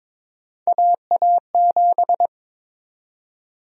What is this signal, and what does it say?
Morse code, heard as a keyed tone, about 22 wpm, 700 Hz. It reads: AA7
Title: USA Call Sign Prefixes - 22wpm